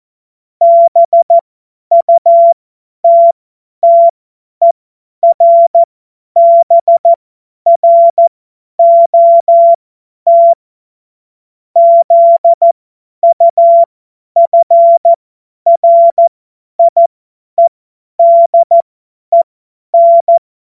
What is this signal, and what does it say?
Morse code, heard as a keyed tone, about 14 wpm, spaced as for 10 wpm, 680 Hz. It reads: BUTTERBROT ZUFRIEDEN
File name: Morsekurs / Witze / Witz 09. Februar 2021, 50 F70